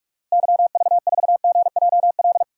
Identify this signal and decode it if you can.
Morse code, heard as a keyed tone, about 41 wpm, 700 Hz. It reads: YV4GJL
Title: DXCC-100 Call Signs - 6 Characters - Repeated 2x - 45wpm